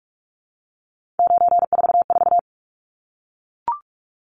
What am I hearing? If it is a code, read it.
Morse code, heard as a keyed tone, about 44 wpm, 700 Hz. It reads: 944
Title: Numbers - 3-Digit 45wpm